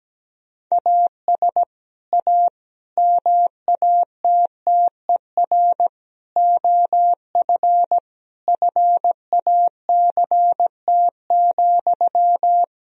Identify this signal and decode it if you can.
Morse code, heard as a keyed tone, about 17 wpm, 700 Hz. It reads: AS A MATTER OF FACT,
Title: Sentences from Top 1000 Words - Review Part 3 - 17wpm